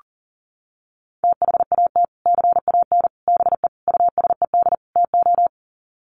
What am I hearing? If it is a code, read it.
Morse code, heard as a keyed tone, about 40 wpm, 700 Hz. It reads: THAT CAN BE USED TO
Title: Sets of 5 Words Encore 40wpm